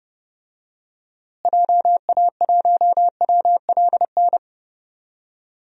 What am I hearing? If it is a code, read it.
Morse code, heard as a keyed tone, about 30 wpm, 700 Hz. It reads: JA1WLN
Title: DXCC-100 Call Signs - 6 Characters - Repeated 2x - 30wpm